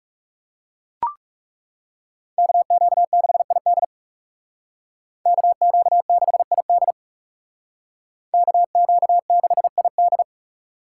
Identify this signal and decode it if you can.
Morse code, heard as a keyed tone, about 38 wpm, 700 Hz. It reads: KQ6ID KQ6ID KQ6ID